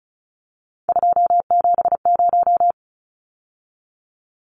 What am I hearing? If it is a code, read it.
Morse code, heard as a keyed tone, about 34 wpm, 700 Hz. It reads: 270
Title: Numbers - 3-Digit 35wpm